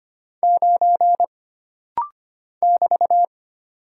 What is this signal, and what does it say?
Morse code, heard as a keyed tone, about 25 wpm, 700 Hz. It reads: 9 <BT>
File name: Single Character (Letter:Number:Punctuation:Prosign) - Rapid-Fire - 25wpm